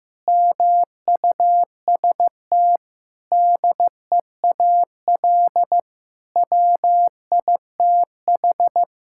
Morse code, heard as a keyed tone, about 15 wpm, 700 Hz. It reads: MUST DEAL WITH
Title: Sentences from Top 1000 Words - Review Part 6 - 15wpm